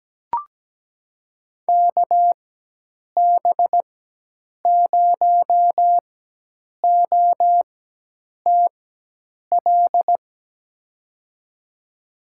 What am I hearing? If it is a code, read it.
Morse code, heard as a keyed tone, about 17 wpm, spaced as for 8 wpm, 700 Hz. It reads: KB0OTL
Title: USA Call Signs - 4x Character Spacing - 17wpm